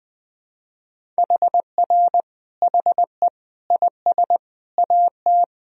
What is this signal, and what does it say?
Morse code, heard as a keyed tone, about 20 wpm, 700 Hz. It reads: HR HE IS AT
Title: Sentences from Top 300 Words and Abbreviations 20wpm